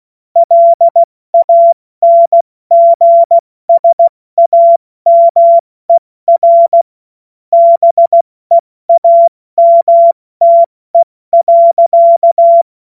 Morse code, heard as a keyed tone, about 16 wpm, 680 Hz. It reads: LANGSAMER BEAMTE.
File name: Witz-21-Oktober-2025-70-F80